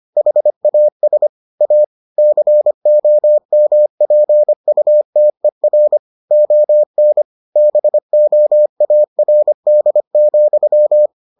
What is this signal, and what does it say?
Morse code, heard as a keyed tone, about 25 wpm, 600 Hz. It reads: HAS A COMPUTER ON BOARD,